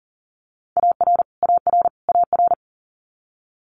Morse code, heard as a keyed tone, about 42 wpm, 700 Hz. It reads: AR AR AR
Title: Multi-letter Phonemes Repeated 3x 40wpm